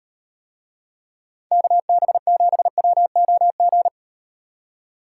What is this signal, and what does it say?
Morse code, heard as a keyed tone, about 38 wpm, 700 Hz. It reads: KB7WOG